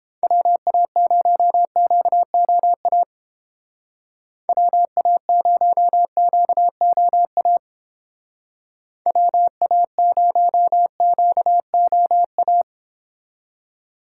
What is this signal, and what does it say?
Morse code, heard as a keyed tone, about 31 wpm, 700 Hz. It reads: WA0QOA WA0QOA WA0QOA